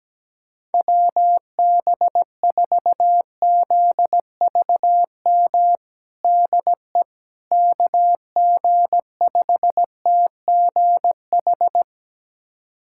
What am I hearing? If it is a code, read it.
Morse code, heard as a keyed tone, about 17 wpm, 700 Hz. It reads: WB4ZVM DE KG5TGH
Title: USA Callsign DE Callsign - 6 Characters - 17wpm